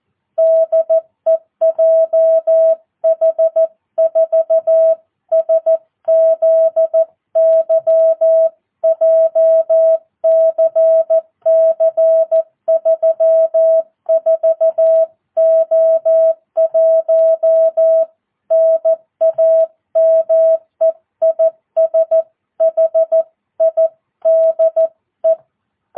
Morse code, heard as a keyed tone, about 14 wpm, 650 Hz. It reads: DE JH4SZY JCC 34O1 NAME IS HIDE
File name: jh8cbh / 2022cw